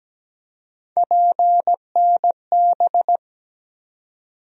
Morse code, heard as a keyed tone, about 17 wpm, 700 Hz. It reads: PNB
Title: USA Call Sign Postfixes - 3 Letters 17wpm